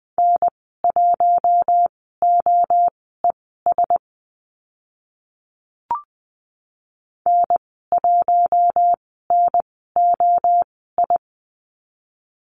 Morse code, heard as a keyed tone, about 20 wpm, spaced as for 14 wpm, 700 Hz. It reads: N1OES N1NOI
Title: 048 Lesson 9g - Number 4 - Call Signs 20wpm